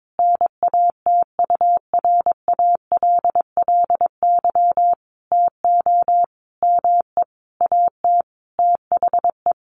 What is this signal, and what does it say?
Morse code, heard as a keyed tone, about 22 wpm, 700 Hz. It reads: NATURALLY TO ME AT THE
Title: Sentences from Top 1300 Words 22wpm